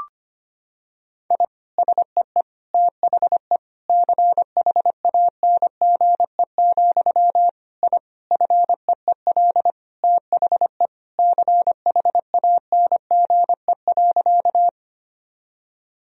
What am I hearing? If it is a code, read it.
Morse code, heard as a keyed tone, about 25 wpm, 700 Hz. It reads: I SEE THE CHANGE, I FEEL THE CHANGE.
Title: Sentences from Top 1000 Words - Review Part 1 - 25wpm